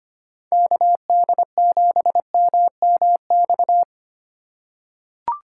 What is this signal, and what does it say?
Morse code, heard as a keyed tone, about 25 wpm, 700 Hz. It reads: KD7MMX
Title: USA Call Signs - 6 Characters - 4x Character Spacing - 25wpm